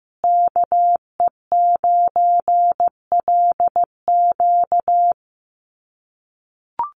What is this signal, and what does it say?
Morse code, heard as a keyed tone, about 15 wpm, 700 Hz. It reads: KE9LQ
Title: USA Call Signs Repeated 3x 15wpm